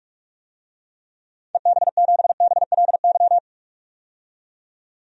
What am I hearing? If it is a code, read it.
Morse code, heard as a keyed tone, about 42 wpm, 700 Hz. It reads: EB7BLY
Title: DXCC-100 Call Signs - 6 Characters - 45wpm